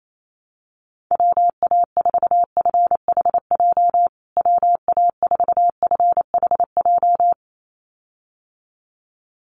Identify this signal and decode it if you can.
Morse code, heard as a keyed tone, about 28 wpm, 700 Hz. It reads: WA4FHJ WA4FHJ
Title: USA Call Signs Repeated 2x 28wpm